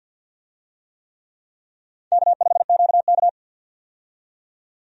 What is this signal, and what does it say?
Morse code, heard as a keyed tone, about 49 wpm, 700 Hz. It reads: K5QK
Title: USA Call Signs - 4 Characters - 50wpm